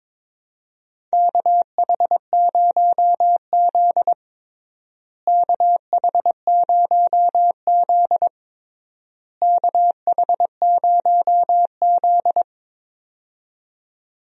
Morse code, heard as a keyed tone, about 22 wpm, 700 Hz. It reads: KH0Z KH0Z KH0Z
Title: USA Call Signs - 4 Characters - Repeated 3x - 22wpm